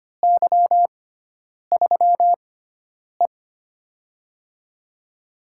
Morse code, heard as a keyed tone, about 25 wpm, spaced as for 9 wpm, 700 Hz. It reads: Y3E
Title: USA Call Signs - 4 Characters - 6x Character Spacing - 25wpm